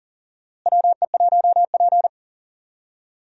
Morse code, heard as a keyed tone, about 40 wpm, 700 Hz. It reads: WE1P
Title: USA Call Signs - 4 Characters - Repeated 3x - 40wpm